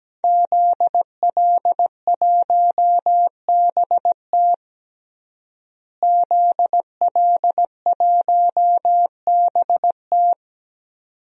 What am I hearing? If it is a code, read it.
Morse code, heard as a keyed tone, about 17 wpm, 700 Hz. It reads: ZL1BT ZL1BT
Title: DXCC-100 Call Signs - 5 Characters - Repeated 2x - 17wpm